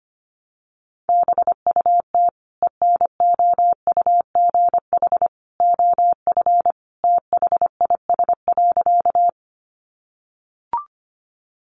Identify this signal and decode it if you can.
Morse code, heard as a keyed tone, about 25 wpm, 700 Hz. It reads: BUT ENOUGH OF THIS.
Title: Sentences from Top 400 Words 25wpm